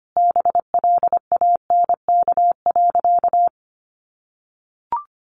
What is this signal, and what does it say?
Morse code, heard as a keyed tone, about 25 wpm, 700 Hz. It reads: BLANK.
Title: Sentences from Top 1500 Words 25wpm